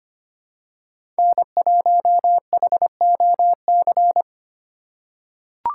Morse code, heard as a keyed tone, about 25 wpm, 700 Hz. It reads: N1HOC
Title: USA Call Signs - 5 Characters - 6x Character Spacing - 25wpm